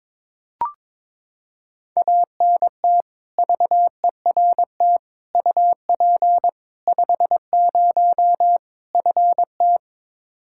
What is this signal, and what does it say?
Morse code, heard as a keyed tone, about 22 wpm, 700 Hz. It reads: ANT VERT UP 50 FT
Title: Practice QSO - Exchange 2 - Info - 22wpm